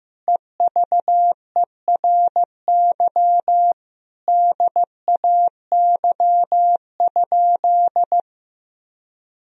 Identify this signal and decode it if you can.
Morse code, heard as a keyed tone, about 15 wpm, 700 Hz. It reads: EVERY DAY?
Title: Sentences from Top 1000 Words - Review Part 4 - 15wpm